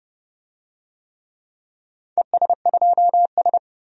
Morse code, heard as a keyed tone, about 32 wpm, 700 Hz. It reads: ES2S
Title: DXCC-100 Call Signs - 4 Characters - 30wpm